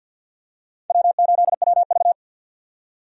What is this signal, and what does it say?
Morse code, heard as a keyed tone, about 52 wpm, 700 Hz. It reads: W8WV
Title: USA Call Signs - 4 Characters - 50wpm